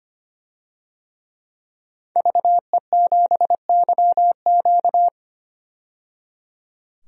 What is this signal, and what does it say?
Morse code, heard as a keyed tone, about 25 wpm, 700 Hz. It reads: VE7YQ
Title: DXCC-100 Call Signs - 5 Characters - 25wpm